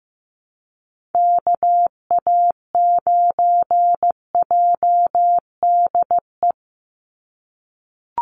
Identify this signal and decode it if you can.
Morse code, heard as a keyed tone, about 15 wpm, 700 Hz. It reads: KA9JDE
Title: USA Call Signs 15wpm